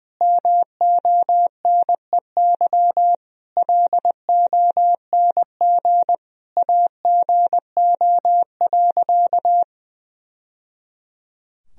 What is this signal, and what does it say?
Morse code, heard as a keyed tone, about 20 wpm, 700 Hz. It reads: MONEY LONG AGO.
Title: Sentences from Top 1000 Words - Review Part 3 - 20wpm